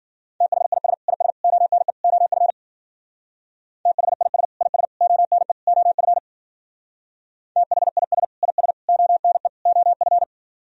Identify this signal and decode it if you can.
Morse code, heard as a keyed tone, about 48 wpm, 700 Hz. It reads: THIS IS ONE OF THIS IS ONE OF THIS IS ONE OF
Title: Speed-Racing - Sets of 4 Words 40wpm